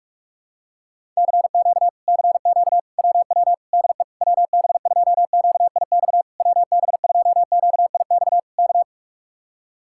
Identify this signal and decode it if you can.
Morse code, heard as a keyed tone, about 48 wpm, 700 Hz. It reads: CQ CQ WW DE WB2QIX WB2QIX K